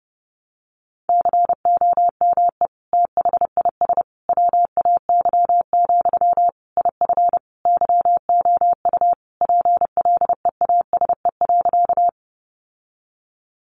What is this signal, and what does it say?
Morse code, heard as a keyed tone, about 30 wpm, 700 Hz. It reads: COME THIS WAY, IF YOU PLEASE.
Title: Sentences from Top 1500 Words 30wpm